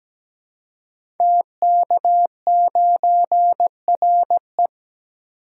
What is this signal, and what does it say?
Morse code, heard as a keyed tone, about 17 wpm, 700 Hz. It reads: TK9RE
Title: Code Groups - 5 Letter:Numbers - 17wpm